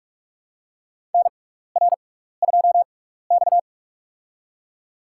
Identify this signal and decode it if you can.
Morse code, heard as a keyed tone, about 46 wpm, spaced as for 16 wpm, 700 Hz. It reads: NR2X
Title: USA Call Signs - 4 Characters - 6x Character Spacing - 45wpm